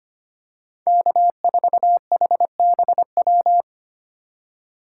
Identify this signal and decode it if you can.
Morse code, heard as a keyed tone, about 25 wpm, 700 Hz. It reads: K4HBW
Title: USA Call Signs - 5 Characters - 4x Character Spacing - 25wpm